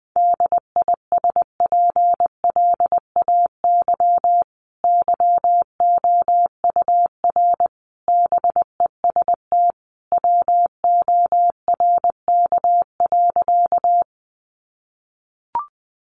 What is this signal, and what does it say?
Morse code, heard as a keyed tone, about 20 wpm, 700 Hz. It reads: DISPLAY YOUR BEST WORK.
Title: Sentences from Top 1200 Words 20wpm